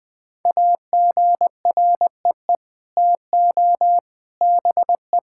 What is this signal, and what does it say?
Morse code, heard as a keyed tone, about 20 wpm, 700 Hz. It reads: AGREE TO BE
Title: Sets of 9 Words - 20wpm